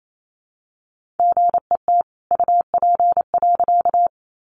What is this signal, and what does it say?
Morse code, heard as a keyed tone, about 28 wpm, 700 Hz. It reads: GET UP.
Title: A Sound Of Thunder 28wpm